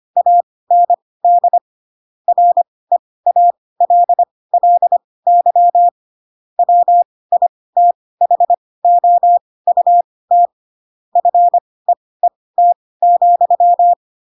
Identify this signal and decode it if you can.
Morse code, heard as a keyed tone, about 25 wpm, spaced as for 18 wpm, 700 Hz. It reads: AND REALLY WITHOUT FEET,
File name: Pinocchio / Pinocchio ch7 12 25 ik1hge